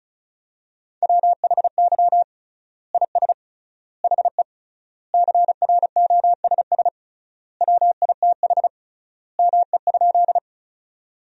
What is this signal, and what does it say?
Morse code, heard as a keyed tone, about 34 wpm, 700 Hz. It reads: WHY IS HE CROSS WITH ME?
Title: 3x Word-Spacing - Sentences from Top 300 Words - 35wpm